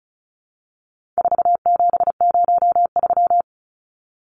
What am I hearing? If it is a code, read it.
Morse code, heard as a keyed tone, about 34 wpm, 700 Hz. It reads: 4703
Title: Numbers - 4-Digit 35wpm